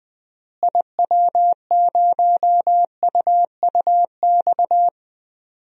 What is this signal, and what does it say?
Morse code, heard as a keyed tone, about 20 wpm, 700 Hz. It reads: IW0UUX
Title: DXCC-100 Call Signs - 6 Characters - Repeated 2x - 20wpm